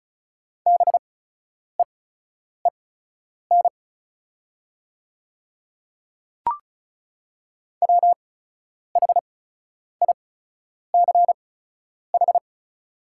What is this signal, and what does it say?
Morse code, heard as a keyed tone, about 36 wpm, spaced as for 10 wpm, 700 Hz. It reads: BEEN WHICH
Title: 8x Character Spacing - Top 100 Words - 35wpm